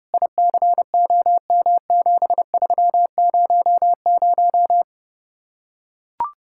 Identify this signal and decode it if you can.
Morse code, heard as a keyed tone, about 32 wpm, 700 Hz. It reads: ICOM7300